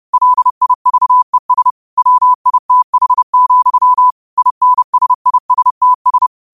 Morse code, heard as a keyed tone, about 30 wpm, 1000 Hz. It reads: LIVES WITH, INSISTS
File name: qotdfiles / 30wpm